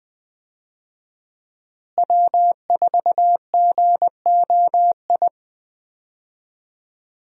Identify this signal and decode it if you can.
Morse code, heard as a keyed tone, about 20 wpm, 700 Hz. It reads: W4GOI
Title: USA Call Signs - 5 Characters - Repeated 3x - 20wpm